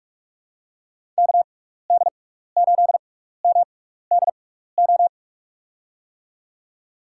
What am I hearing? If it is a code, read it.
Morse code, heard as a keyed tone, about 46 wpm, spaced as for 15 wpm, 700 Hz. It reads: KD8MDO
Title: USA Call Signs - 6 Characters - 6x Character Spacing - 45wpm